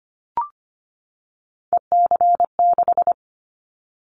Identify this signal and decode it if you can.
Morse code, heard as a keyed tone, about 25 wpm, 700 Hz. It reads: EC6
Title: Callsign Prefixes for Top 100 DXCC - 25wpm